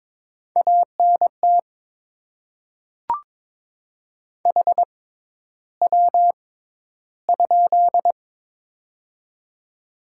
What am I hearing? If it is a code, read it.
Morse code, heard as a keyed tone, about 22 wpm, spaced as for 11 wpm, 700 Hz. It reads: ANT HW?
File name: Top 20 QSO Elements - 6x Character Spacing - 22wpm